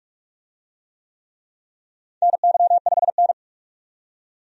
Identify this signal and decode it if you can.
Morse code, heard as a keyed tone, about 42 wpm, 700 Hz. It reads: NY5N
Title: USA Call Signs - 4 Characters - 45wpm